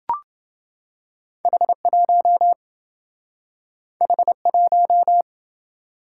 Morse code, heard as a keyed tone, about 27 wpm, 700 Hz. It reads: H1 H1